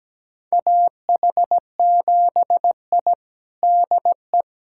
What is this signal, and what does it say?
Morse code, heard as a keyed tone, about 17 wpm, 700 Hz. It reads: AH7I DE
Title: USA Callsign DE Callsign - 4 Characters - 17wpm